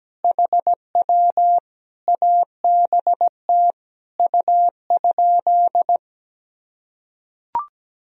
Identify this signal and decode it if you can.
Morse code, heard as a keyed tone, about 17 wpm, 700 Hz. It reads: HW ABT U?
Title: Sentences from Top 100 Words and Abbreviations 17wpm